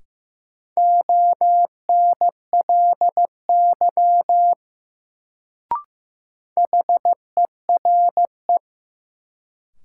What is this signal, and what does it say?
Morse code, heard as a keyed tone, about 15 wpm, 700 Hz. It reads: ONLY HERE
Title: Sentences from Top 200 Words - New Word Drill - 15wpm